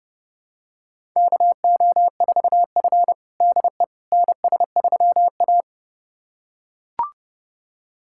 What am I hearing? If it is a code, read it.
Morse code, heard as a keyed tone, about 32 wpm, 700 Hz. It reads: KO4F DE NS3A
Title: USA Callsign DE Callsign - 4 Characters - 30wpm